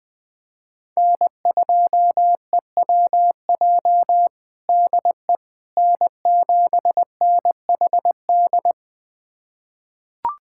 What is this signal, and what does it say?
Morse code, heard as a keyed tone, about 20 wpm, 700 Hz. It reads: N2EWJ DE N7NHD
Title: USA Callsign DE Callsign - 5 Characters - 20wpm